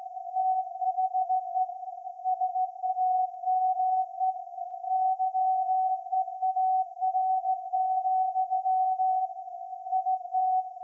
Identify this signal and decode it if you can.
Morse code, heard as a keyed tone, about 17 wpm, 750 Hz. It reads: THE SAME YEAR, IT
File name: iz7slz / VLF / SAQ 30-06-2024 0900